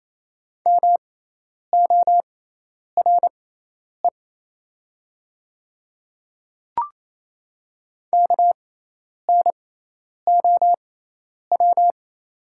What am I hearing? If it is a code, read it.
Morse code, heard as a keyed tone, about 28 wpm, spaced as for 10 wpm, 700 Hz. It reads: MORE KNOW
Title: 6x Character Spacing - Top 200 Words - 28wpm